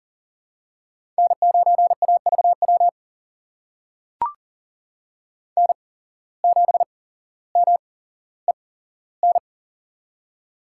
Morse code, heard as a keyed tone, about 40 wpm, spaced as for 17 wpm, 700 Hz. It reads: N9AVW N7MEN
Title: USA Call Signs - 5 Characters - 8x Character Spacing - 40wpm